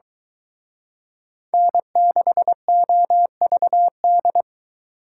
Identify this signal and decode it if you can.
Morse code, heard as a keyed tone, about 23 wpm, 700 Hz. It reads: N6OVD